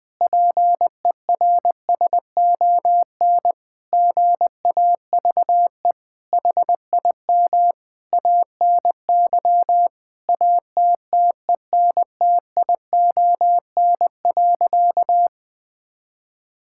Morse code, heard as a keyed tone, about 20 wpm, 700 Hz. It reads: PERSON GAVE HIM ANY ATTENTION.
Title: Sentences from Top 1000 Words - Review Part 6 - 20wpm